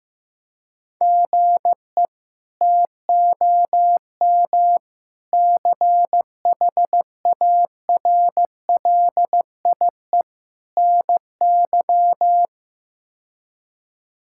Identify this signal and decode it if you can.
Morse code, heard as a keyed tone, about 15 wpm, 700 Hz. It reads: GE TOM CHARLIE NY
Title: K1USN SST QSO Practice - 15wpm